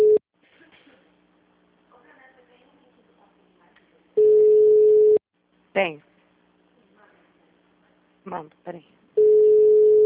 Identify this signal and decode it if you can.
Ringback tone (Europe, 425 Hz), with speech.